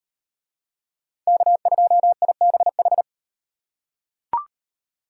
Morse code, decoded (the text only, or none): K2IBH